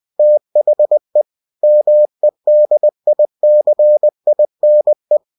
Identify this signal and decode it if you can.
Morse code, heard as a keyed tone, about 20 wpm, 600 Hz. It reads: THE MEDICINE